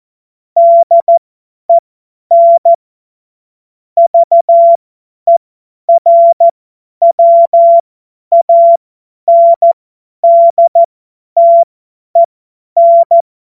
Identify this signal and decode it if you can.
Morse code, heard as a keyed tone, about 14 wpm, spaced as for 10 wpm, 680 Hz. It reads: DEN VERWANDTEN
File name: Morsekurs / Witze / Witz 17. Mai 2022 50 F70